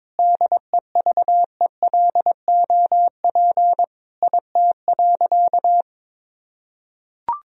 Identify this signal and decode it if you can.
Morse code, heard as a keyed tone, about 22 wpm, 700 Hz. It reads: DEVELOP IT.
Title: Sentences from Top 1000 Words - Review Part 2 - 22wpm